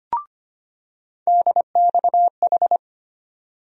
Morse code, heard as a keyed tone, about 25 wpm, 700 Hz. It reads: DXH
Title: USA Call Sign Postfixes - 3 Letters 25wpm